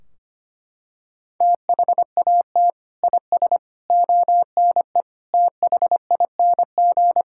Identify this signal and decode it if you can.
Morse code, heard as a keyed tone, about 25 wpm, 700 Hz. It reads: THAT IS ONE THING
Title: Sentences from Top 1600 Words 25wpm